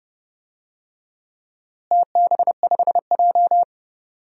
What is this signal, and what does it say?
Morse code, heard as a keyed tone, about 30 wpm, 700 Hz. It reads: TB5J
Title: DXCC-100 Call Signs - 4 Characters - Repeated 2x - 30wpm